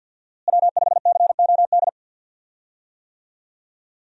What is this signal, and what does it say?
Morse code, heard as a keyed tone, about 52 wpm, 700 Hz. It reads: W5COD
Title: USA Call Signs - 5 Characters - Repeated 3x - 50wpm